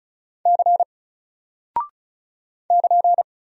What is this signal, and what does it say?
Morse code, heard as a keyed tone, about 36 wpm, 700 Hz. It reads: C <KN>
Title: Single Character (Letter:Number:Punctuation:Prosign) 35wpm